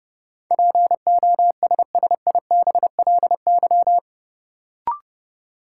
Morse code, heard as a keyed tone, about 30 wpm, 700 Hz. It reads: POSSIBLY
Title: Sentences from Top 1300 Words - New Word Drill - 30wpm